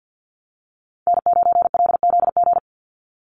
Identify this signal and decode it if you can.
Morse code, heard as a keyed tone, about 50 wpm, 700 Hz. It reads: N9LZG
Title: USA Call Signs 50wpm